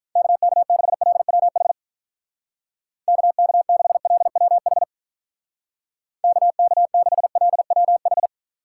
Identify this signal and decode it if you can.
Morse code, heard as a keyed tone, about 44 wpm, 700 Hz. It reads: KK6LWH KK6LWH KK6LWH